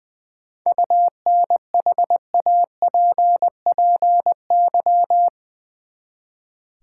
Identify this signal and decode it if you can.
Morse code, heard as a keyed tone, about 20 wpm, 700 Hz. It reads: UNHAPPY
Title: Seven Letter Words - 20wpm